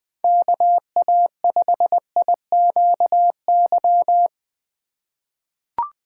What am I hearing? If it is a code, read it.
Morse code, heard as a keyed tone, about 20 wpm, 700 Hz. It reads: KA5IQY
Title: USA Call Signs - 6 Characters - 4x Character Spacing - 20wpm